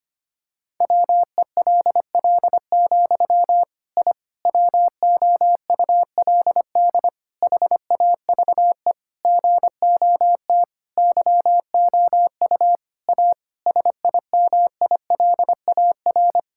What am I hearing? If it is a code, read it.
Morse code, heard as a keyed tone, about 25 wpm, 700 Hz. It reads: WELL, I WOULD HAVE GOT YOU A SIMILAR
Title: Sentences from Top 1000 Words - Review Part 5 - 25wpm